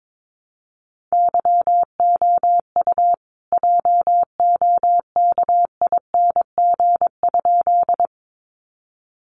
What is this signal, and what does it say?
Morse code, heard as a keyed tone, about 22 wpm, 700 Hz. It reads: YOU JOKING?
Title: A Sound Of Thunder 22wpm